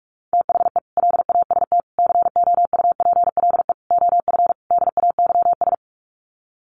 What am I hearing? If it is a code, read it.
Morse code, heard as a keyed tone, about 44 wpm, 700 Hz. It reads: THE LAST COUPLE OF DAYS
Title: Sets of 5 Words Encore 45wpm